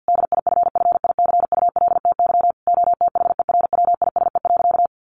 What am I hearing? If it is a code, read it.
Morse code, heard as a keyed tone, about 49 wpm, 700 Hz. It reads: DIFFICULTY OTHERWISE.
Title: Sentences from Top 1500 Words 50wpm